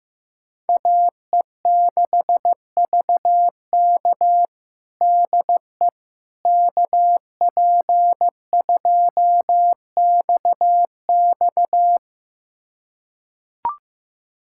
Morse code, heard as a keyed tone, about 16 wpm, 700 Hz. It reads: AE6VK DE KP2XX
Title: USA Callsign DE Callsign - 5 Characters - 15wpm